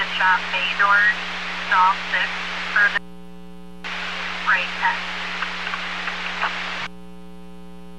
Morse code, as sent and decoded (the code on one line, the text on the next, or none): none